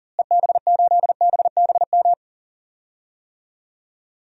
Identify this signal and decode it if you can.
Morse code, heard as a keyed tone, about 40 wpm, 700 Hz. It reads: EB8BBM
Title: DXCC-100 Call Signs - 6 Characters - Repeated 2x - 40wpm